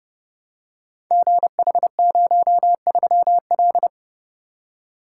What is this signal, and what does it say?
Morse code, heard as a keyed tone, about 30 wpm, 700 Hz. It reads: GH03L
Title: Code Groups - 5 Letter:Numbers - 30wpm